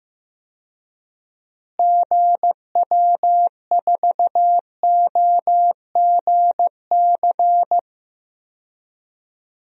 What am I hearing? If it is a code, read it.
Morse code, heard as a keyed tone, about 15 wpm, 700 Hz. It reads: GW4OGC
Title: DXCC-100 Call Signs - 6 Characters - Repeated 3x - 15wpm